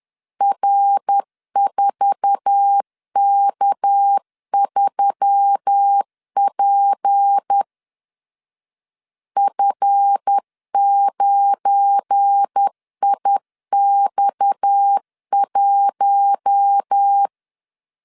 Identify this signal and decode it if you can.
Morse code, heard as a keyed tone, about 10 wpm, 800 Hz. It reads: R4K3P F9IX1
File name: CW / DET / 1 9 1500C S B 10wpm 4439